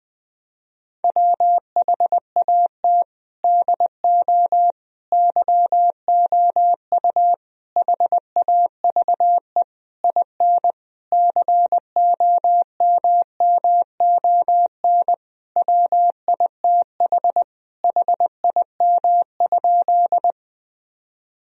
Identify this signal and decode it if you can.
Morse code, heard as a keyed tone, about 21 wpm, 700 Hz. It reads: WHAT DO YOU HAVE IN COMMON WITH HIM?
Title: Sentences from Top 1000 Words - Review Part 3 - 20wpm